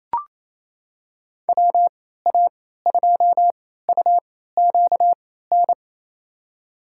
Morse code, heard as a keyed tone, about 28 wpm, spaced as for 16 wpm, 700 Hz. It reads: WA2UQN
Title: USA Call Signs - 6 Characters - 3x Character Spacing - 28wpm